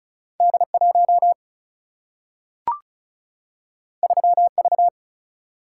Morse code, heard as a keyed tone, about 36 wpm, 700 Hz. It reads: D 1 3 V
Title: Single-Letter Single-Number 35wpm